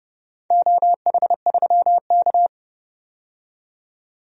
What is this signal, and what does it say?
Morse code, heard as a keyed tone, about 29 wpm, 700 Hz. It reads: OH3K
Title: DXCC-100 Call Signs - 4 Characters - 30wpm